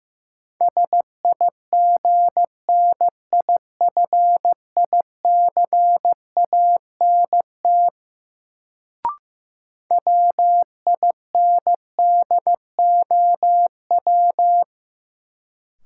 Morse code, heard as a keyed tone, about 15 wpm, 700 Hz. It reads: SIGNIFICANT WINDOW
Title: Sentences from Top 1600 Words - New Word Drill Alt - 15wpm